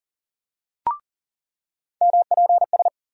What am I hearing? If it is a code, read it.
Morse code, heard as a keyed tone, about 40 wpm, 700 Hz. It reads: MPS
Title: USA Call Sign Postfixes - 3 Letters 40wpm